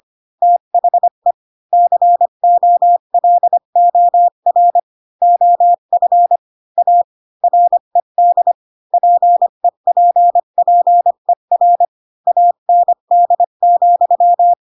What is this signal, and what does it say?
Morse code, heard as a keyed tone, about 25 wpm, 700 Hz. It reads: THE COLOR OF A RED PEPPER AND,